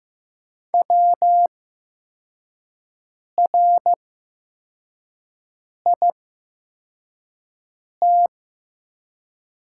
Morse code, heard as a keyed tone, about 15 wpm, spaced as for 4 wpm, 700 Hz. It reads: WRIT
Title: 8x Character Spacing - Top 100 Words - 15wpm